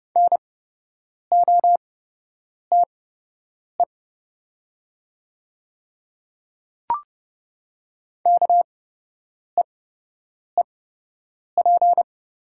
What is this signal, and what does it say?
Morse code, heard as a keyed tone, about 30 wpm, spaced as for 8 wpm, 690 Hz. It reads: NOTE KEEP
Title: 8x Character Spacing - Top 500 Words - 30wpm